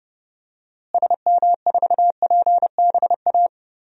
Morse code, heard as a keyed tone, about 30 wpm, 700 Hz. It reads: SM4PBA
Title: DXCC-100 Call Signs - 6 Characters - Repeated 3x - 30wpm